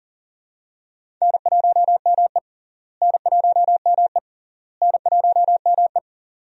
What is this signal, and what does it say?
Morse code, heard as a keyed tone, about 38 wpm, 700 Hz. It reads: N1ME N1ME N1ME
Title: USA Call Signs - 4 Characters - Repeated 3x - 40wpm